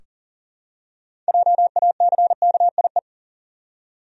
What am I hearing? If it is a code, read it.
Morse code, heard as a keyed tone, about 40 wpm, 700 Hz. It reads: JACKIE
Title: USA Common Names - 40wpm